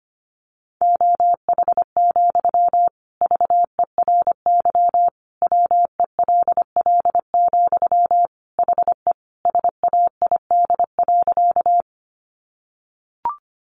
Morse code, heard as a keyed tone, about 25 wpm, 700 Hz. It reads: OH, VERY WELL, HE SAID.
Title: Sentences from Top 500 Words 25wpm